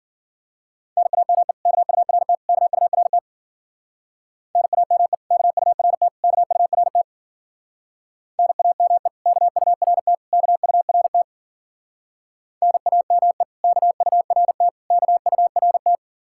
Morse code, heard as a keyed tone, about 48 wpm, 700 Hz. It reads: NAME KURT KURT NAME KURT KURT NAME KURT KURT NAME KURT KURT
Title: Speed-Racing Common QSO Phrases - 40wpm